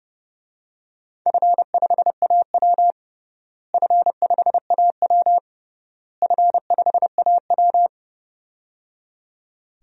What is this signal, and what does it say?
Morse code, heard as a keyed tone, about 30 wpm, 700 Hz. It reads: F5AW F5AW F5AW
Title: DXCC-100 Call Signs - 4 Characters - Repeated 3x - 30wpm